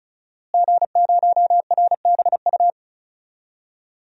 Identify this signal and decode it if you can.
Morse code, heard as a keyed tone, about 36 wpm, 700 Hz. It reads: G0RBU
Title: DXCC-100 Call Signs - 5 Characters - Repeated 3x - 35wpm